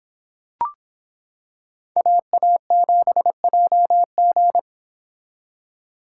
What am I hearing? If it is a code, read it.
Morse code, heard as a keyed tone, about 26 wpm, 700 Hz. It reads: AA7JG